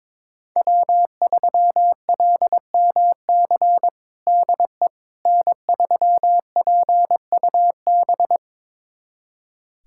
Morse code, heard as a keyed tone, about 22 wpm, 700 Hz. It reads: W3LMC DE N3PUB
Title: USA Callsign DE Callsign - 5 Characters - 22wpm